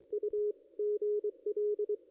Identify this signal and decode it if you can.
Morse code, heard as a keyed tone, about 23 wpm, 440 Hz, with radio noise. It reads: U GL
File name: contests / ss-cw-k5tr-op-n5rz -0400